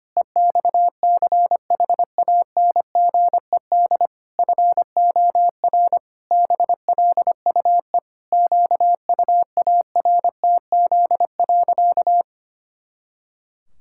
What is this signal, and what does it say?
Morse code, heard as a keyed tone, about 25 wpm, 700 Hz. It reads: EXCHANGED FOR BLUE QUARTZ.
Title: Pangram Challenge v2 - 25wpm